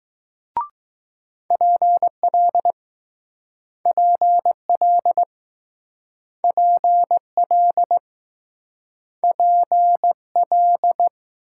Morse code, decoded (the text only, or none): PL PL PL PL